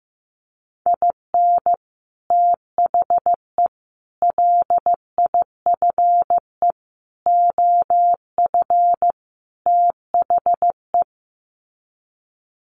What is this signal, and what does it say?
Morse code, heard as a keyed tone, about 15 wpm, 700 Hz. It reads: IN THE LIFE OF THE
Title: Sets of 5 Words Encore2 15wpm